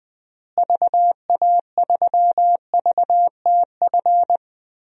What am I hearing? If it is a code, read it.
Morse code, heard as a keyed tone, about 20 wpm, 700 Hz. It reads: VA3VTF
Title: DXCC-100 Call Signs - 6 Characters - Repeated 2x - 20wpm